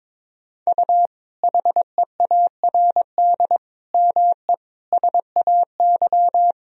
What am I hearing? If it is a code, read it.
Morse code, heard as a keyed tone, about 22 wpm, 700 Hz. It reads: U HEARD ME SAY
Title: Sentences from Top 500 Words and Abbreviations 22wpm